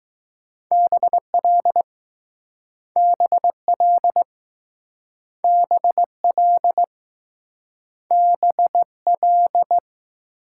Morse code, heard as a keyed tone, about 20 wpm, 700 Hz. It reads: BL BL BL BL